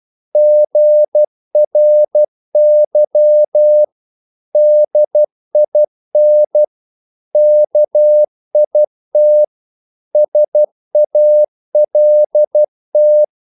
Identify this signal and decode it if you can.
Morse code, heard as a keyed tone, about 12 wpm, 600 Hz. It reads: GRY DIN KIT SALT
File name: Korte ord DK 12wpm